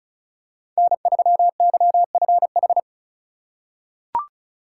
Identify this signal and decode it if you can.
Morse code, heard as a keyed tone, about 36 wpm, 690 Hz. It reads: N3YFH
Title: USA Call Signs - 5 Characters - 3x Character Spacing - 35wpm